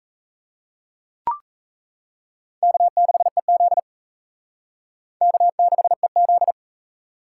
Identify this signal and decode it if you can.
Morse code, heard as a keyed tone, about 38 wpm, 700 Hz. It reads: K6EZ K6EZ